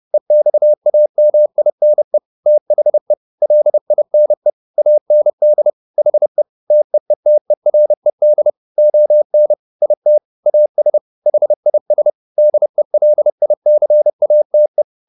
Morse code, heard as a keyed tone, about 30 wpm, 600 Hz. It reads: EXAMINE THE LINE AND HE TEETERED ON IT AS HIS DELICATE
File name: the old man and the sea / the old man and the sea 1 30wpm